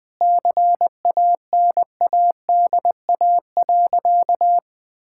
Morse code, heard as a keyed tone, about 21 wpm, 700 Hz. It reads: CANADA.
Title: Sentences with 5-Word Phrases - 20wpm